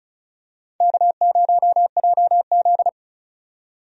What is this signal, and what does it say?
Morse code, heard as a keyed tone, about 36 wpm, 700 Hz. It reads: K0JZ
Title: USA Call Signs - 4 Characters - 35wpm